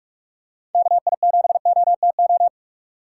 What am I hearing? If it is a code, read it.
Morse code, heard as a keyed tone, about 43 wpm, 700 Hz. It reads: KI7OTO
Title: USA Call Signs - 6 Characters - 2x Character Spacing - 45wpm